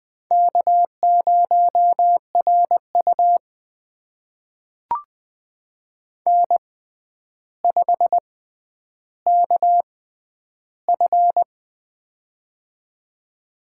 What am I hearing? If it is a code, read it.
Morse code, heard as a keyed tone, about 20 wpm, spaced as for 10 wpm, 700 Hz. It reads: K0RU N5KF
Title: USA Call Signs - 6x Character Spacing - 20wpm